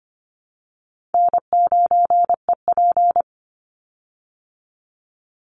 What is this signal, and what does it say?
Morse code, heard as a keyed tone, about 25 wpm, 700 Hz. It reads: N9EP
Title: Easy Call Signs Repeated 3x - 25wpm